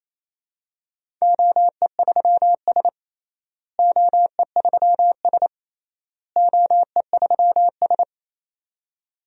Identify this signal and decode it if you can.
Morse code, heard as a keyed tone, about 28 wpm, 700 Hz. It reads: OE3S OE3S OE3S
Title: DXCC-100 Call Signs - 4 Characters - Repeated 3x - 28wpm